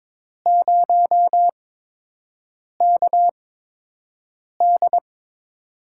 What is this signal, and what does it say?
Morse code, heard as a keyed tone, about 22 wpm, spaced as for 6 wpm, 700 Hz. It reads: 0KD
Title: USA Call Signs - 4 Characters - 8x Character Spacing - 22wpm